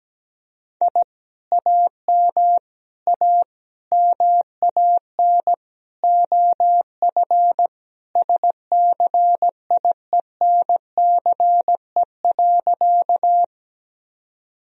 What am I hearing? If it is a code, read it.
Morse code, heard as a keyed tone, about 17 wpm, 700 Hz. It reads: I AM A MAN OF SCIENCE.
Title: Sentences from Top 1000 Words - Review Part 2 - 17wpm